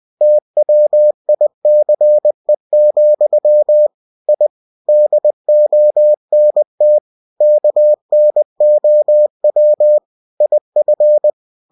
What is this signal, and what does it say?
Morse code, heard as a keyed tone, about 20 wpm, 600 Hz. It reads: TWICE, I DONT KNOW IF
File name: cw-books / winnie-the-pooh / winnie-20wpm-page-0066